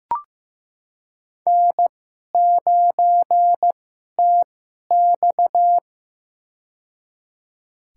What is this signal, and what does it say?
Morse code, heard as a keyed tone, about 15 wpm, spaced as for 11 wpm, 700 Hz. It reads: N9TX
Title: USA Call Signs - 2x Character Spacing - 15wpm